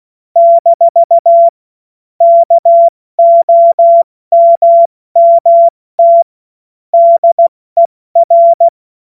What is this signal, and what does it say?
Morse code, heard as a keyed tone, about 16 wpm, 680 Hz. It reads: - KOMMT DER
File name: Morsekurs / Witze / Witz 09. November 2021, 70 F80